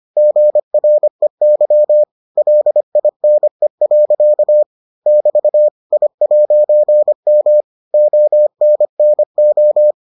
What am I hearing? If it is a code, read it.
Morse code, heard as a keyed tone, about 25 wpm, 600 Hz. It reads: GREY LINE. <BT> I'M ONNO